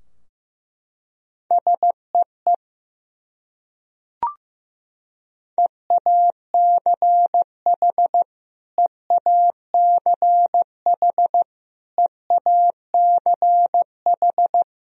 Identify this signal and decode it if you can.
Morse code, heard as a keyed tone, about 15 wpm, 700 Hz. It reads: SEE EACH EACH EACH
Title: Top 100 Words Repeated 15wpm